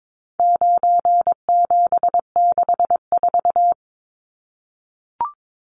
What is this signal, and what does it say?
Morse code, heard as a keyed tone, about 22 wpm, 700 Hz. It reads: 9764
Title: Numbers - 4-Digit 22wpm